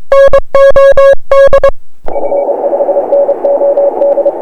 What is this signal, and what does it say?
Morse code, heard as a keyed tone, about 26 wpm, 560 Hz. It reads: NOD F5NOD
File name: dxrecords / st2a 80cw